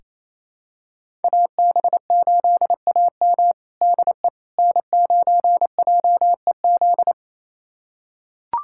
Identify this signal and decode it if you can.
Morse code, heard as a keyed tone, about 28 wpm, 700 Hz. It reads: AB8AM DE N9JEZ
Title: USA Callsign DE Callsign - 5 Characters - 28wpm